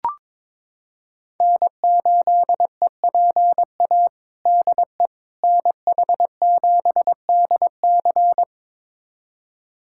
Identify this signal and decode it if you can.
Morse code, heard as a keyed tone, about 22 wpm, 700 Hz. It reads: N8EPA DE NH7DC
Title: USA Callsign DE Callsign - 5 Characters - 22wpm